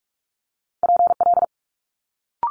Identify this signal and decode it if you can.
Morse code, heard as a keyed tone, about 46 wpm, 700 Hz. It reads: PL
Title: Multi-letter Phonemes Quick Fire 45wpm